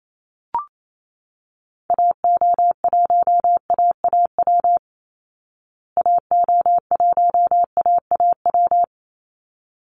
Morse code, heard as a keyed tone, about 28 wpm, 700 Hz. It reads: AO1AAW AO1AAW
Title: Call Signs Repeated 3x - 28wpm